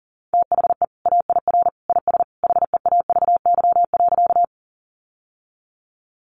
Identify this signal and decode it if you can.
Morse code, heard as a keyed tone, about 40 wpm, 700 Hz. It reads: THE AIR IS HEAVY.
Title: Sentences from Top 1100 Words 40wpm